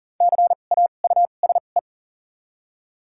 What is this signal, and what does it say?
Morse code, heard as a keyed tone, about 40 wpm, spaced as for 29 wpm, 690 Hz. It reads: CAUSE
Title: 2x Character Spacing - Top 500 Words - 40wpm